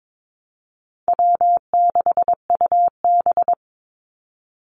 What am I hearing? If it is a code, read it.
Morse code, heard as a keyed tone, about 22 wpm, 700 Hz. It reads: W6UB
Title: NA SOTA Activator Callsigns 22wpm